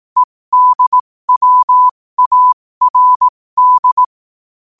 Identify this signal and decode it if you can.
Morse code, heard as a keyed tone, about 18 wpm, spaced as for 15 wpm, 1000 Hz. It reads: EDWARD
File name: qotdfiles / 15wpm